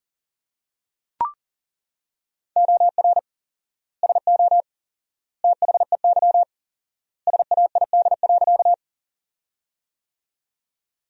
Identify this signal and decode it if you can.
Morse code, heard as a keyed tone, about 40 wpm, 700 Hz. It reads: OR SO THEY SAID.
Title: 4x Word-Spacing - Sentences from Top 100 Words - 40wpm